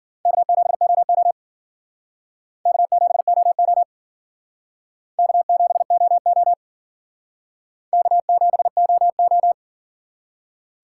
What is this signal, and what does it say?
Morse code, heard as a keyed tone, about 54 wpm, 700 Hz. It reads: K7OO K7OO K7OO K7OO